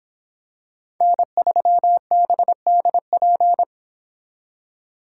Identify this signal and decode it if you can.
Morse code, heard as a keyed tone, about 25 wpm, 700 Hz. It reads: N3BDP